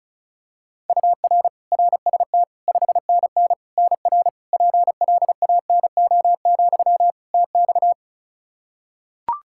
Morse code, heard as a keyed tone, about 34 wpm, 700 Hz. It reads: UR RST 5NN NR PLANO, TX
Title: Practice QSO - Exchange 1 - Beginning - 35wpm